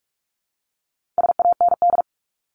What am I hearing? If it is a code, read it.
Morse code, heard as a keyed tone, about 44 wpm, 700 Hz. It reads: SAND
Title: Top 1501-1750 Words 45wpm